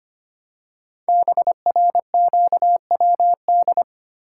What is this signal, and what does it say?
Morse code, heard as a keyed tone, about 25 wpm, 700 Hz. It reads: BRQWD
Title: Code Groups - 5 Letters - 25wpm